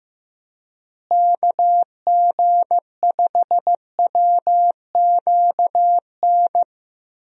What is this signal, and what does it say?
Morse code, heard as a keyed tone, about 16 wpm, 700 Hz. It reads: KG5WQN
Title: USA Call Signs - 6 Characters - Repeated 3x - 15wpm